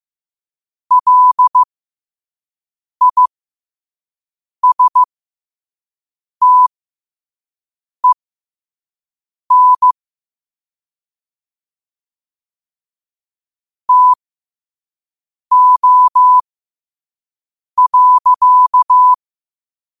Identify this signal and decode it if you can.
Morse code, heard as a keyed tone, about 15 wpm, spaced as for 5 wpm, 1000 Hz. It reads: LISTEN TO.
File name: qotdfiles / 5wpm